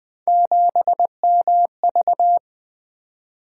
Morse code, heard as a keyed tone, about 20 wpm, 700 Hz. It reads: 7MV
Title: Three Letter-Number Combinations - 20wpm